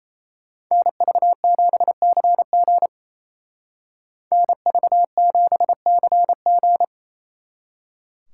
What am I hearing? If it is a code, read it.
Morse code, heard as a keyed tone, about 37 wpm, 700 Hz. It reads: NV7CG NV7CG